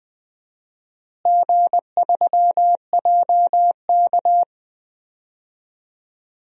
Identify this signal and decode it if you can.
Morse code, heard as a keyed tone, about 20 wpm, 700 Hz. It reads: G3JK
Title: DXCC-100 Call Signs - 4 Characters - Repeated 2x - 20wpm